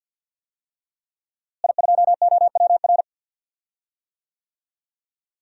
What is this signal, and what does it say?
Morse code, heard as a keyed tone, about 49 wpm, 700 Hz. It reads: I2OWR
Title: DXCC-100 Call Signs - 5 Characters - Repeated 2x - 50wpm